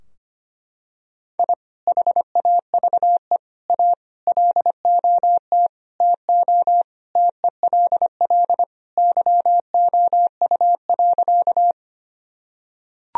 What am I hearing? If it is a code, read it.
Morse code, heard as a keyed tone, about 25 wpm, 700 Hz. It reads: I HAVE A LOT TO TELL YOU.
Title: Sentences from Top 1000 Words - Review Part 3 - 25wpm